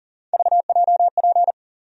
Morse code, heard as a keyed tone, about 40 wpm, 700 Hz. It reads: VJP
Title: USA Call Sign Postfixes - 3 Letters 40wpm